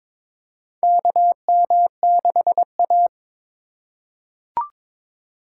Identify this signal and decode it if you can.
Morse code, heard as a keyed tone, about 22 wpm, 700 Hz. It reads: KM6A
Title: USA Call Signs - 4 Characters - 4x Character Spacing - 22wpm